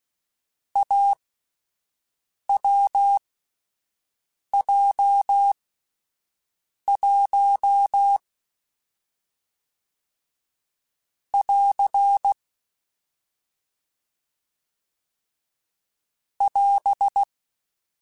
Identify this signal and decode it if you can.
Morse code, heard as a keyed tone, about 16 wpm, spaced as for 5 wpm, 800 Hz. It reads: AWJ1 <AR> <AS>
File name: Difficult Combos / Difficult Combos-mp3s / AWJ1 @5